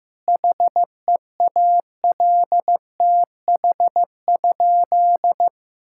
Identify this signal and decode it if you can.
Morse code, heard as a keyed tone, about 15 wpm, 700 Hz. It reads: HEALTH?
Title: Sentences from Top 1000 Words - Review Part 5 - 15wpm